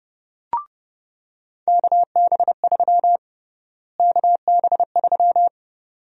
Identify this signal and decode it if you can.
Morse code, heard as a keyed tone, about 29 wpm, 700 Hz. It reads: KB3 KB3
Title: USA Call Sign Prefixes - Repeated 2x - 30wpm